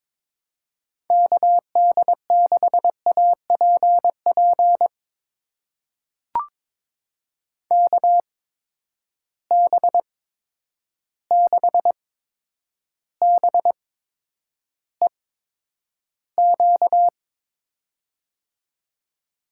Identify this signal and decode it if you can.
Morse code, heard as a keyed tone, about 22 wpm, spaced as for 9 wpm, 700 Hz. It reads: KD6APP KB6BEQ
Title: USA Call Signs - 6 Characters - 8x Character Spacing - 22wpm